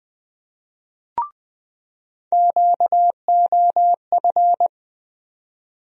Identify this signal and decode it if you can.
Morse code, heard as a keyed tone, about 20 wpm, 700 Hz. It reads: QOF
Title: USA Call Sign Postfixes - 3 Letters 20wpm